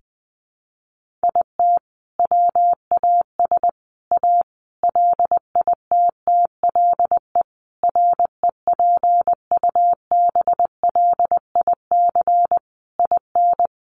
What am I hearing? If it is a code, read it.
Morse code, heard as a keyed tone, about 20 wpm, 700 Hz. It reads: IT WAS A LITTLE REPUBLIC IN
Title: Sentences from Top 1500 Words 20wpm